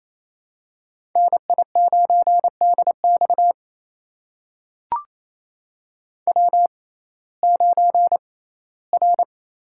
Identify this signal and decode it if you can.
Morse code, heard as a keyed tone, about 28 wpm, spaced as for 17 wpm, 700 Hz. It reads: NI9DX W9R
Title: USA Call Signs - 5 Characters - 6x Character Spacing - 28wpm